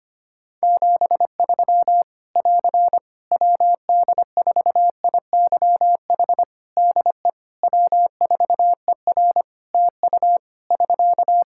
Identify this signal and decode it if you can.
Morse code, heard as a keyed tone, about 25 wpm, 700 Hz. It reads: 73 <AR> WD4IYH DE W4ER TU <SK>
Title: Practice QSO - Exchange 4 - Ending - 25wpm